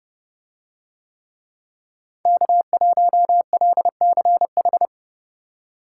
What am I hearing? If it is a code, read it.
Morse code, heard as a keyed tone, about 30 wpm, 700 Hz. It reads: K1LCH